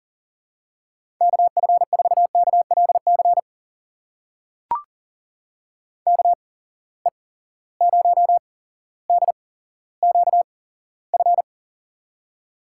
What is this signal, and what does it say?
Morse code, heard as a keyed tone, about 40 wpm, spaced as for 17 wpm, 700 Hz. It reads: KF4KLC KE0DQF
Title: USA Call Signs - 6 Characters - 8x Character Spacing - 40wpm